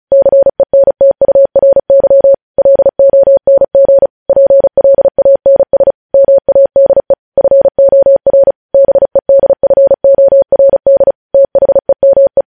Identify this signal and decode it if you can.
Morse code, heard as a keyed tone, about 36 wpm, 550 Hz. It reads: CENTURY LONG PLANS MADE FOR BEDFORD THEME
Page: news headlines in morse code at 35 wpm